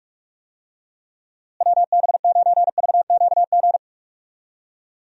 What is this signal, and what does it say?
Morse code, heard as a keyed tone, about 41 wpm, 700 Hz. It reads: WB9VQG
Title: USA Call Signs - 6 Characters - Repeated 2x - 45wpm